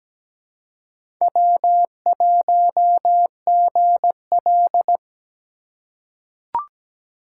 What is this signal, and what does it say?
Morse code, heard as a keyed tone, about 17 wpm, 700 Hz. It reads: W1GL
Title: USA Call Signs - 4 Characters - 3x Character Spacing - 17wpm